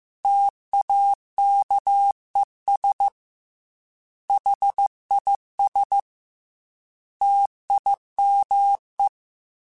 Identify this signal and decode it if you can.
Morse code, heard as a keyed tone, about 15 wpm, 800 Hz. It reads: TAKES HIS TIME
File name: T then H Runs / T then H Runs.mp3s / T then H Runs @13